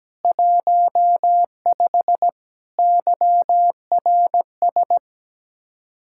Morse code, heard as a keyed tone, about 17 wpm, 700 Hz. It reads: 15 YRS
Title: Speed-Racing Common QSO Phrases Encore - 17wpm